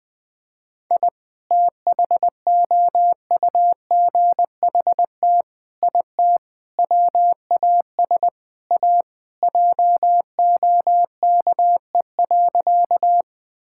Morse code, heard as a keyed tone, about 20 wpm, 700 Hz. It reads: I THOUGHT IT WAS A JOKE.
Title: Sentences with 5-Word Phrases - 20wpm